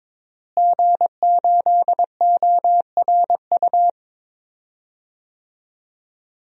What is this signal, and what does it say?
Morse code, heard as a keyed tone, about 23 wpm, 700 Hz. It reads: G8ORU
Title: DXCC-100 Call Signs - 5 Characters - Repeated 2x - 22wpm